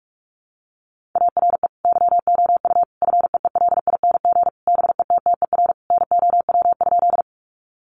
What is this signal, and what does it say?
Morse code, heard as a keyed tone, about 44 wpm, 700 Hz. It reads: ARE YOU FEELING BETTER NOW?
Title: Sentences from Top 1200 Words 45wpm